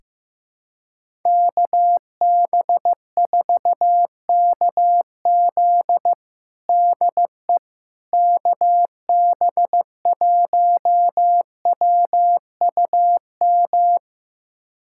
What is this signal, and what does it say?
Morse code, heard as a keyed tone, about 15 wpm, 700 Hz. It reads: KB4KZ DE KB1WUM
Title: USA Callsign DE Callsign 15wpm